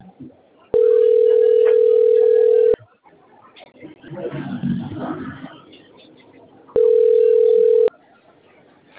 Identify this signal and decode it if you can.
Ringback tone (North America), 440 + 480 Hz.